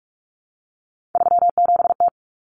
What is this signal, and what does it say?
Morse code, heard as a keyed tone, about 44 wpm, 700 Hz. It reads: 37T
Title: Common Cut Numbers - 3-Digit 45wpm